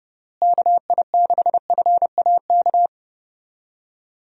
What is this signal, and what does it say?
Morse code, heard as a keyed tone, about 32 wpm, 700 Hz. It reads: KI6FAK
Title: USA Call Signs - 6 Characters - Repeated 3x - 30wpm